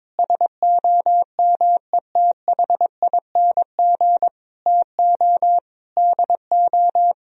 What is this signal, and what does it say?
Morse code, heard as a keyed tone, about 22 wpm, 700 Hz. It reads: SOMETHING TO DO
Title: Sentences from Top 1000 Words - Review Part 2 - 22wpm